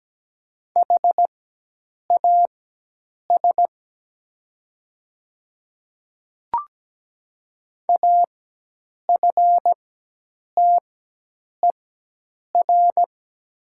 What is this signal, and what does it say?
Morse code, heard as a keyed tone, about 17 wpm, spaced as for 8 wpm, 700 Hz. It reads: HAS AFTER
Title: 4x Character Spacing - Top 300 Words - 17wpm